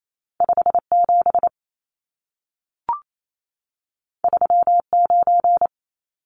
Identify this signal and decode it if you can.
Morse code, heard as a keyed tone, about 28 wpm, 700 Hz. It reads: 57 39
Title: Numbers - 2-Digit 28wpm